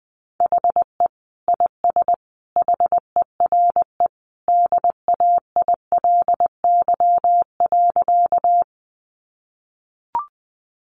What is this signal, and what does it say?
Morse code, heard as a keyed tone, about 20 wpm, 700 Hz. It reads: HE IS HERE DAILY.
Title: Sentences from Top 1100 Words 20wpm